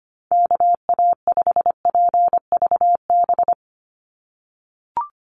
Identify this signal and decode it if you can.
Morse code, heard as a keyed tone, about 25 wpm, 700 Hz. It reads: KA5PVB
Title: NA SOTA Chaser Callsigns 25wpm